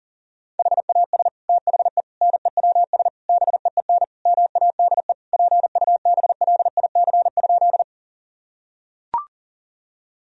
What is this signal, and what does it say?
Morse code, heard as a keyed tone, about 40 wpm, 700 Hz. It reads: HAS THE NEWS BEEN MADE PUBLIC?
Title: Sentences from Top 1000 Words - Review Part 3 - 40wpm